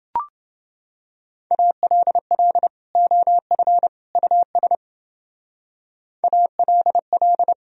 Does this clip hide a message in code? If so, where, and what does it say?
Yes: Morse code from 1.5 to 7.5 s, heard as a keyed tone, about 29 wpm, 700 Hz. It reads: ALL OF US ALL